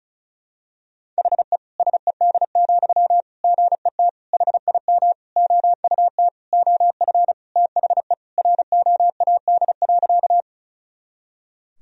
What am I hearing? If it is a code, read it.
Morse code, heard as a keyed tone, about 36 wpm, 700 Hz. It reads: HE SED, GET HIM OUT OF THE ROAD.
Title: Sentences from Top 400 Words and Abbreviations 35wpm